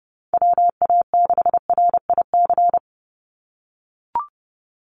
Morse code, heard as a keyed tone, about 30 wpm, 700 Hz. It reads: WA6RIC
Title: NA SOTA Chaser Callsigns 30wpm